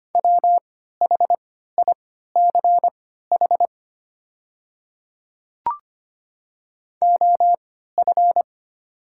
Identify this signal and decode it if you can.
Morse code, heard as a keyed tone, about 25 wpm, spaced as for 14 wpm, 700 Hz. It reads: WHICH OF
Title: 3x Character Spacing - Top 100 Words - 25wpm